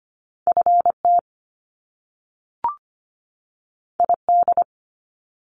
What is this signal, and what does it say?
Morse code, heard as a keyed tone, about 25 wpm, 700 Hz. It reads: FT ID
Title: Top Two Letter Combinations in Words 25wpm